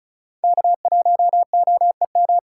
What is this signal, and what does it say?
Morse code, heard as a keyed tone, about 34 wpm, 700 Hz. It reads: K1OEM
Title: USA Call Signs - 5 Characters - 4x Character Spacing - 35wpm